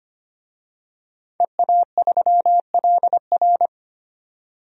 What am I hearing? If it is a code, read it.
Morse code, heard as a keyed tone, about 25 wpm, 700 Hz. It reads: EA3LR
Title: DXCC-100 Call Signs - 5 Characters - 25wpm